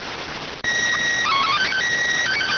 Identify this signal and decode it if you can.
SSTV (slow-scan television), PD 290.